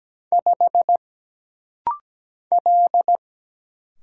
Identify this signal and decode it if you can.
Morse code, heard as a keyed tone, about 17 wpm, 700 Hz. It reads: 5 L
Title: Single Letter-Number Alt - Rapid-Fire 17wpm